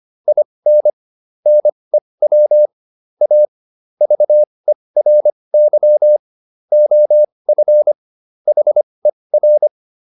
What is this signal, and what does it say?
Morse code, heard as a keyed tone, about 25 wpm, spaced as for 20 wpm, 600 Hz. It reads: IN NEW A VERY OF HER